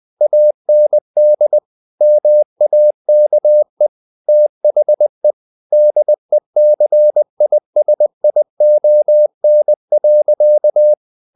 Morse code, heard as a keyed tone, about 20 wpm, 600 Hz. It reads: AND MAKE THE DECISION.
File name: the old man and the sea part 2 20wpm